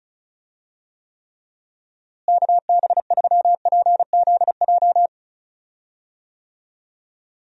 Morse code, heard as a keyed tone, about 36 wpm, 700 Hz. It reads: KB3PZJ
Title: USA Call Signs - 6 Characters - Repeated 2x - 35wpm